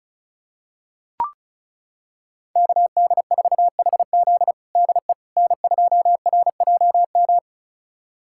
Morse code, heard as a keyed tone, about 36 wpm, 700 Hz. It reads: KD4HZ DE N2RJM
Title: USA Callsign DE Callsign - 5 Characters - 35wpm